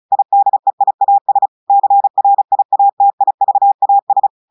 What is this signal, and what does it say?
Morse code, heard as a keyed tone, about 36 wpm, 800 Hz. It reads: IDEIAS CRIATIVAS
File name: portugais texte en clair 2 vitesse 35 mots minutes